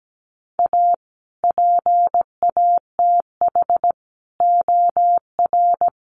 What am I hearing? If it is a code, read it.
Morse code, heard as a keyed tone, about 17 wpm, 700 Hz. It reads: A PATH OR
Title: Sentences from Top 1400 Words 17wpm